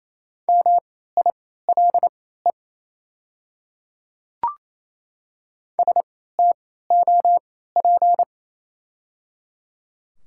Morse code, heard as a keyed tone, about 28 wpm, spaced as for 16 wpm, 700 Hz. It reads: MILE STOP
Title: 3x Character Spacing - Top 300 Words - 28wpm